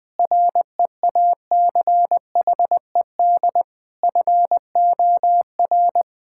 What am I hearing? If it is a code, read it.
Morse code, heard as a keyed tone, about 21 wpm, 700 Hz. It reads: REACHED FOR
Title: Sentences with 7-Word Phrases - Spoken Beginning - 20wpm